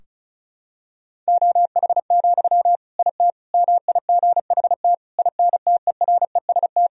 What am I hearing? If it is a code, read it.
Morse code, heard as a keyed tone, about 36 wpm, 700 Hz. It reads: OH, IT MIGHT INTEREST
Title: Sentences from Top 1000 Words - Review Part 2 - 35wpm